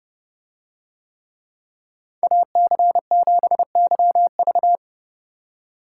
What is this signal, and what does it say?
Morse code, heard as a keyed tone, about 30 wpm, 700 Hz. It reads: AC7YV